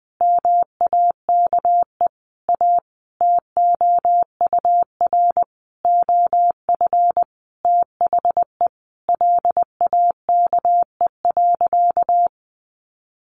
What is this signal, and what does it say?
Morse code, heard as a keyed tone, about 20 wpm, 700 Hz. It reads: MAKE A TOUR OF THE LAKE.
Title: Sentences from Top 1400 Words 20wpm